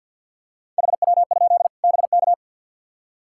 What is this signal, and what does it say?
Morse code, heard as a keyed tone, about 53 wpm, 700 Hz. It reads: HW? BK
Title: Practice QSO - Exchange 2 - Info - 50wpm